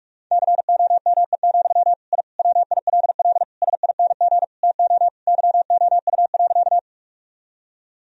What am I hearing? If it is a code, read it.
Morse code, heard as a keyed tone, about 41 wpm, 700 Hz. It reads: COME, I WILL SING TO YOU.
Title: Sentences from Top 1000 Words - Review Part 2 - 45wpm